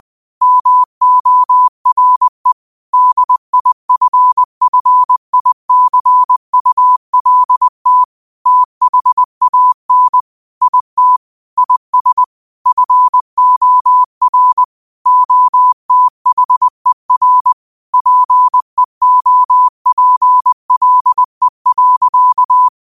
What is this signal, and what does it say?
Morse code, heard as a keyed tone, about 20 wpm, 1000 Hz. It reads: MORE DIFFICULT THAN IT IS FOR OTHER PEOPLE.
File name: qotdfiles / 20wpm